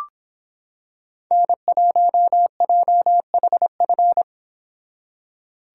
Morse code, heard as a keyed tone, about 26 wpm, 700 Hz. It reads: N1JHF